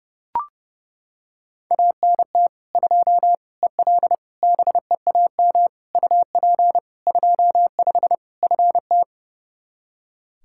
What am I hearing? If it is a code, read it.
Morse code, heard as a keyed tone, about 30 wpm, 700 Hz. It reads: ANT 2 EL BEAM UP 25 FT
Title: Practice QSO - Exchange 2 - Info - 30wpm